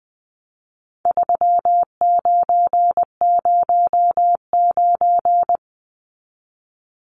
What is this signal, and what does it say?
Morse code, heard as a keyed tone, about 20 wpm, 700 Hz. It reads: 3909
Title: Numbers - 4-Digit 20wpm